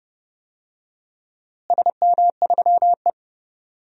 Morse code, heard as a keyed tone, about 29 wpm, 700 Hz. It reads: SM3E
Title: DXCC-100 Call Signs - 4 Characters - Repeated 3x - 30wpm